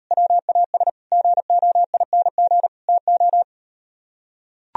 Morse code, heard as a keyed tone, about 38 wpm, 690 Hz. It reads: WAS GOING TO